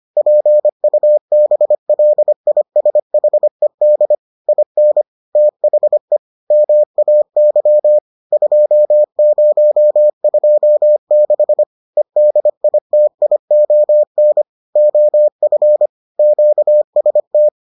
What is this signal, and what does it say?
Morse code, heard as a keyed tone, about 25 wpm, 600 Hz. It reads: PUBLISHED IN THE MAY 2026 EDITION OF QST